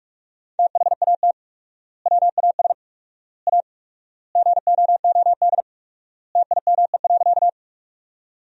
Morse code, heard as a keyed tone, about 41 wpm, 700 Hz. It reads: THAT WAS A GOOD TIME.
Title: 4x Word-Spacing - Sentences from Top 200 Words - 45wpm